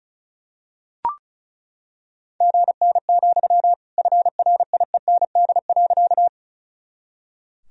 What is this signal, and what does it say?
Morse code, heard as a keyed tone, about 34 wpm, 700 Hz. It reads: GN, FRIEND.
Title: Sentences from Top 300 Words and Abbreviations 35wpm